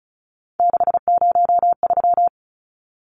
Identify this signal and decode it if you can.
Morse code, heard as a keyed tone, about 34 wpm, 700 Hz. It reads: 603
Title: Numbers - 3-Digit 35wpm